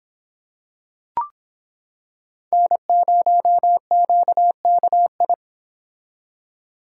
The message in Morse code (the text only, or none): N0QKI